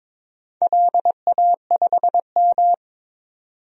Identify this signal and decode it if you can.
Morse code, heard as a keyed tone, about 22 wpm, 700 Hz. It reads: LA5M
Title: DXCC-100 Call Signs - 4 Characters - 22wpm